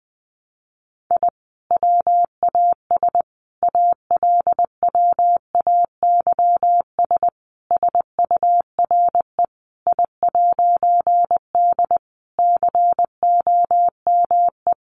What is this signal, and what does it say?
Morse code, heard as a keyed tone, about 19 wpm, 700 Hz. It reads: I WAS ALWAYS SURE I'D COME
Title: A Sound Of Thunder 20wpm